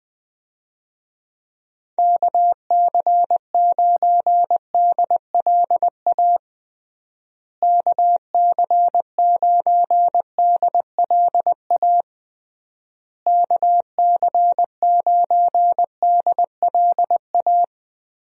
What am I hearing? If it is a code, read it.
Morse code, heard as a keyed tone, about 21 wpm, 700 Hz. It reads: KC9DLA KC9DLA KC9DLA
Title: USA Call Signs - 6 Characters - Repeated 3x - 20wpm